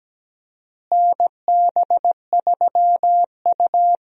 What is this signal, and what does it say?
Morse code, heard as a keyed tone, about 17 wpm, 700 Hz. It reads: NB3U
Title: USA Call Signs - 4 Characters - Repeated 2x - 17wpm